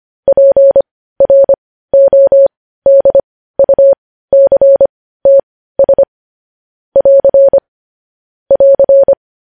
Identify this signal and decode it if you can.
Morse code, heard as a keyed tone, about 25 wpm, spaced as for 15 wpm, 550 Hz. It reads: PRODUCTS <AR> <AR>
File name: podcast / News headlines-1x-25x15